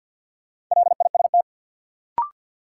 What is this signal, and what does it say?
Morse code, heard as a keyed tone, about 52 wpm, 700 Hz. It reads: LIST
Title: Sentences from Top 400 Words - New Word Drill - 50wpm